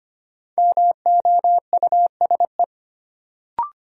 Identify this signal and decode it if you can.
Morse code, heard as a keyed tone, about 25 wpm, 700 Hz. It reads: MOUSE
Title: Sentences from Top 700 Words - New Word Drill - 25wpm